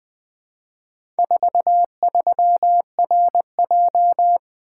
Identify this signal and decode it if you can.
Morse code, heard as a keyed tone, about 20 wpm, 700 Hz. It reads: 43RJ
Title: Four Letter-Number Combinations - 20wpm